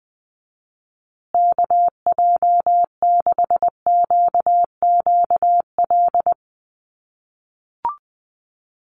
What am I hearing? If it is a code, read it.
Morse code, heard as a keyed tone, about 20 wpm, 700 Hz. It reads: KJ6QQL
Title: USA Call Signs Repeated 3x 20wpm